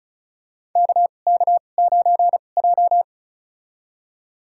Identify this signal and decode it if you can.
Morse code, heard as a keyed tone, about 36 wpm, spaced as for 25 wpm, 700 Hz. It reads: KK9J
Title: USA Call Signs - 4 Characters - 2x Character Spacing - 35wpm